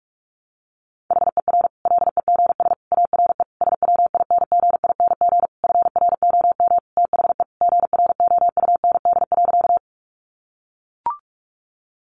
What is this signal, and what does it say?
Morse code, heard as a keyed tone, about 44 wpm, 700 Hz. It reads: HER LEGS ARE SWINGING FROM THE GROUND.
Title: Easy Phrases 45wpm